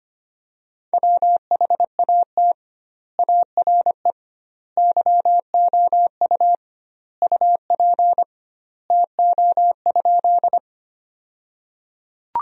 Morse code, heard as a keyed tone, about 25 wpm, 700 Hz. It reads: WHAT ARE YOU UP TO?
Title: 2x Word-Spacing - Sentences from Top 100 Words - 25wpm